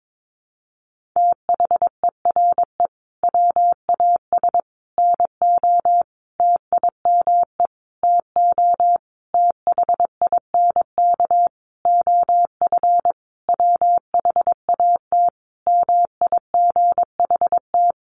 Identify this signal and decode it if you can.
Morse code, heard as a keyed tone, about 22 wpm, 700 Hz. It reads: THERE WAS NO TIME TO THINK OF WHAT MIGHT
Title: Sentences from Top 400 Words 22wpm